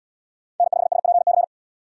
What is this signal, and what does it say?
Morse code, heard as a keyed tone, about 71 wpm, 700 Hz. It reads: N5SPC